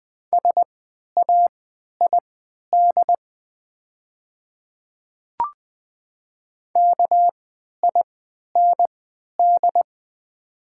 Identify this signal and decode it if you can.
Morse code, heard as a keyed tone, about 21 wpm, spaced as for 11 wpm, 700 Hz. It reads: SAID KIND
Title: 3x Character Spacing - Top 300 Words - 20wpm